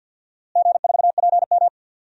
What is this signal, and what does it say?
Morse code, heard as a keyed tone, about 47 wpm, 700 Hz. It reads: G4PM
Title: DXCC-100 Call Signs - 4 Characters - Repeated 3x - 50wpm